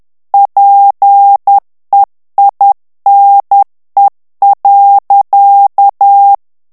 Morse code, heard as a keyed tone, about 10 wpm, 800 Hz. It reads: PEINE.
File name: morse mp3 / FRANCAIS TEXTE CLAIR 2 VITESSE 10 MOTS MINUTES